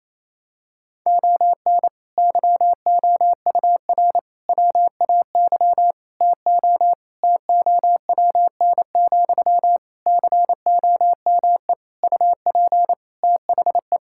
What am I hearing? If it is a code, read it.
Morse code, heard as a keyed tone, about 28 wpm, 700 Hz. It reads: ON YOUR WAY TO TOWN, COME UP THE
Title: Sentences from Top 1000 Words - Review Part 6 - 28wpm